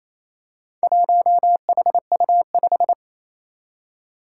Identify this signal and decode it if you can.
Morse code, heard as a keyed tone, about 28 wpm, 700 Hz. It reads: 1HU5
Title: Four Challenging Letter-Number Combinations 28wpm